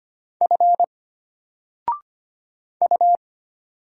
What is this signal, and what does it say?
Morse code, heard as a keyed tone, about 25 wpm, 700 Hz. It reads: F U
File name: Single-Letter-Number-25wpm